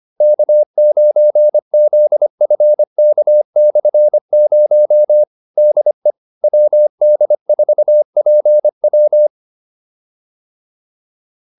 Morse code, heard as a keyed tone, about 25 wpm, 600 Hz. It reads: K9ZFK/0 DE WD4PW
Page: advanced-practice-files